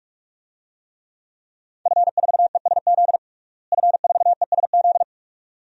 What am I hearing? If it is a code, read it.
Morse code, heard as a keyed tone, about 46 wpm, 700 Hz. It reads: F4ESZ F4ESZ
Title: DXCC-100 Call Signs - 5 Characters - Repeated 2x - 45wpm